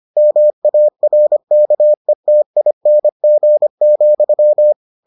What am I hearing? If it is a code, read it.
Morse code, heard as a keyed tone, about 25 wpm, 600 Hz. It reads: MARKETING,